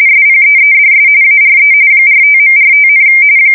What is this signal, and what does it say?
RTTY, about 45.45 baud. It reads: SPEEDS ARE 45.45,
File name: fsk170 / teletype-RTTY